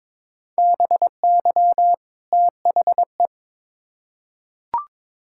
Morse code, heard as a keyed tone, about 21 wpm, 700 Hz. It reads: BY THE
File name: Build A Phrase - 2-Word Phrases - 22wpm